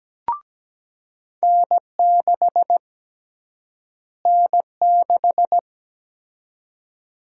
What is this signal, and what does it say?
Morse code, heard as a keyed tone, about 17 wpm, 700 Hz. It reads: N6 N6
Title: USA Call Sign Prefixes - Repeated 2x - 17wpm